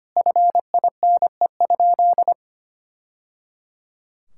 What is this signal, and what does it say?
Morse code, heard as a keyed tone, about 25 wpm, 700 Hz. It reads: FINE?
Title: Sentences from Top 1000 Words - Review Part 5 - 25wpm